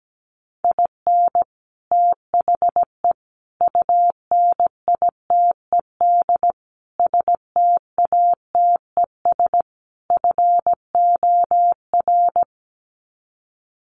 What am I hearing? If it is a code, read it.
Morse code, heard as a keyed tone, about 17 wpm, 700 Hz. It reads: IN THE UNITED STATES FOR
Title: Sets of 5 Words Encore2 17wpm